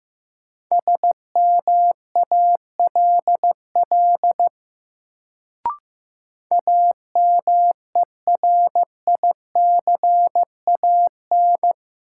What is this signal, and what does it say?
Morse code, heard as a keyed tone, about 15 wpm, 700 Hz. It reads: SMALL AMERICAN
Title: Sets of 2 Words - New Word Drill - 15wpm